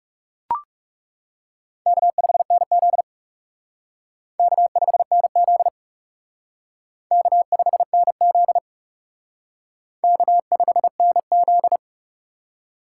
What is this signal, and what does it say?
Morse code, heard as a keyed tone, about 38 wpm, 700 Hz. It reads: K5NZ K5NZ K5NZ K5NZ